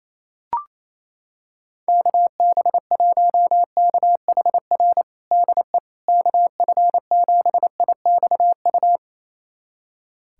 Morse code, heard as a keyed tone, about 28 wpm, 700 Hz. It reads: KB1KHR DE KF7IXU
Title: USA Callsign DE Callsign - 6 Characters - 28wpm